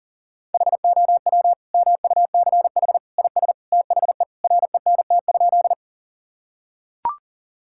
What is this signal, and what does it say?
Morse code, heard as a keyed tone, about 40 wpm, 700 Hz. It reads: HOW MUCH IS THE RENT?
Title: Sentences from Top 1000 Words - Review Part 5 - 40wpm